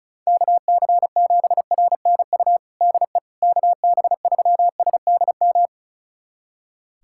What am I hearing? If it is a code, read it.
Morse code, heard as a keyed tone, about 34 wpm, 700 Hz. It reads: KC7RNU DE KB3SDM
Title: USA Callsign DE Callsign - 6 Characters - 35wpm